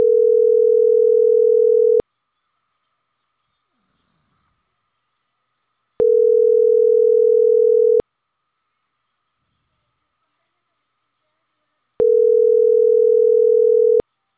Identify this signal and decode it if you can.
Ringback tone (North America), 440 + 480 Hz.